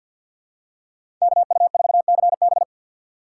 Morse code, heard as a keyed tone, about 50 wpm, 700 Hz. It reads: KU4CB
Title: USA Call Signs - 5 Characters - 3x Character Spacing - 50wpm